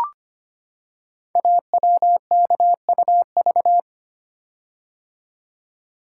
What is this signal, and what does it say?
Morse code, heard as a keyed tone, about 25 wpm, 700 Hz. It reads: AWKUV
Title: Code Groups - 5 Letters - 25wpm